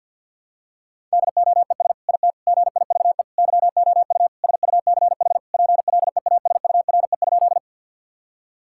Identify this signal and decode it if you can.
Morse code, heard as a keyed tone, about 50 wpm, 700 Hz. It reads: DOES IT GIVE YOU SUCH PLEASURE?
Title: Sentences from Top 1600 Words 50wpm